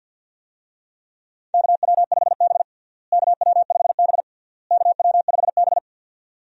Morse code, heard as a keyed tone, about 51 wpm, 700 Hz. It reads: KW5B KW5B KW5B
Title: USA Call Signs - 4 Characters - Repeated 3x - 50wpm